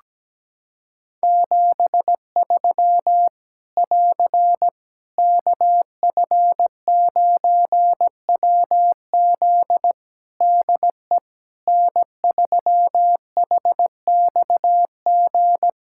Morse code, heard as a keyed tone, about 17 wpm, 700 Hz. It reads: 73 <AR> KF9WZ DE N3HXG
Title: Practice QSO - Exchange 4 - Ending - 17wpm